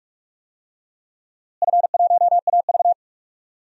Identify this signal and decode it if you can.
Morse code, heard as a keyed tone, about 43 wpm, 700 Hz. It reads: F1AV
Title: DXCC-100 Call Signs - 4 Characters - Repeated 3x - 45wpm